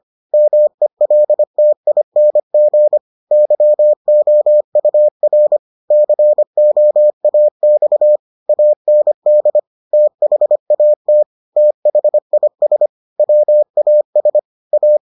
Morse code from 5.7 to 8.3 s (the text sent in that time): COAX